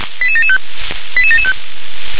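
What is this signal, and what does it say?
Five-tone selective call (ZVEI), heard as quick runs of five tones, tones of about 70 ms. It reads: 88775 88775
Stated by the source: ZVEI five-tone selective call received on SDR.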